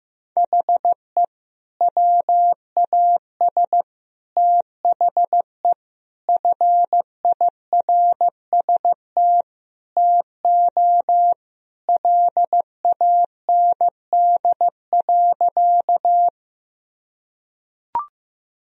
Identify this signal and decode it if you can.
Morse code, heard as a keyed tone, about 16 wpm, 700 Hz. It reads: HE WAS THE FIRST TO LAND.
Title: Sentences from Top 1000 Words - Review Part 1 - 15wpm